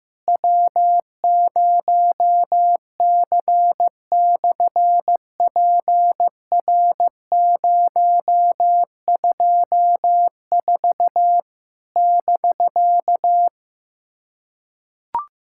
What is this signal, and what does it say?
Morse code, heard as a keyed tone, about 14 wpm, 700 Hz. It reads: W0C/PR024 <BK>
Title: SOTA QSO Practice 15wpm